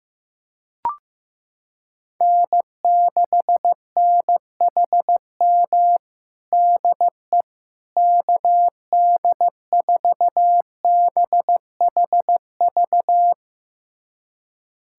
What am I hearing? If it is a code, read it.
Morse code, heard as a keyed tone, about 16 wpm, 700 Hz. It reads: N6NHM DE KD4BHV
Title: USA Callsign DE Callsign 15wpm